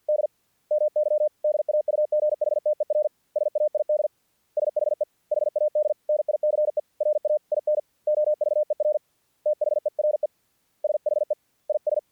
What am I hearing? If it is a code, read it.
Morse code, heard as a keyed tone, about 48 wpm, 600 Hz. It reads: D MY DAUGHTER SAID SHE HAD NICE RAIN OVER THERE SHE IS
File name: archived / CRSnet